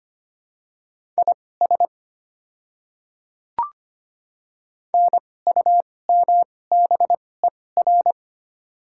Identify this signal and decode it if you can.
Morse code, heard as a keyed tone, about 25 wpm, spaced as for 18 wpm, 700 Hz. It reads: IS NUMBER
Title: 2x Character Spacing - Top 100 Words - 25wpm